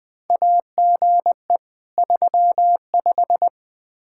Morse code, heard as a keyed tone, about 20 wpm, 700 Hz. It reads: AGE 35
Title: Practice QSO - Exchange 3 - More Info - 20wpm